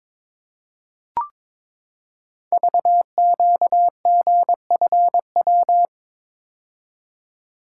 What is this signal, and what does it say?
Morse code, heard as a keyed tone, about 22 wpm, 690 Hz. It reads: VQGFW
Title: Code Groups - 5 Letters - 22wpm